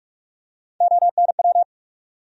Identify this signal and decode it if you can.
Morse code, heard as a keyed tone, about 42 wpm, 700 Hz. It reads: ONW
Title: USA Call Sign Postfixes - 3 Letters 45wpm